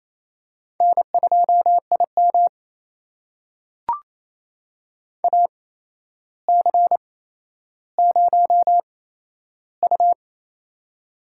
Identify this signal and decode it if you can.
Morse code, heard as a keyed tone, about 28 wpm, spaced as for 12 wpm, 700 Hz. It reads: N2IM AC0U
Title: USA Call Signs - 8x Character Spacing - 28wpm